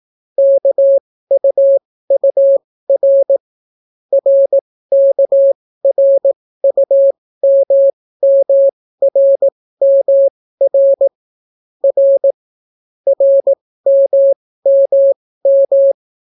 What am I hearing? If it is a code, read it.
Morse code, heard as a keyed tone, about 18 wpm, spaced as for 14 wpm, 550 Hz. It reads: KUUR RKRUMMRMR R RMMM